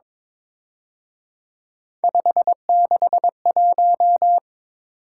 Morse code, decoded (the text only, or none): 561